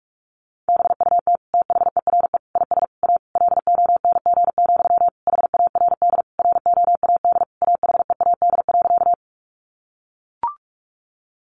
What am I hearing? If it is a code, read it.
Morse code, heard as a keyed tone, about 44 wpm, 700 Hz. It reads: BUT THERE IS A LONG, HARD ROAD AHEAD.
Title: Sentences from Top 1100 Words 45wpm